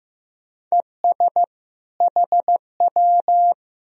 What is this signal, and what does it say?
Morse code, heard as a keyed tone, about 15 wpm, 700 Hz. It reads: ES HW
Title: Sentences from Top 200 Words and Abbreviations 15wpm